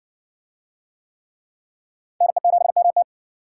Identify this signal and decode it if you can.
Morse code, heard as a keyed tone, about 60 wpm, 700 Hz. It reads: NE7MT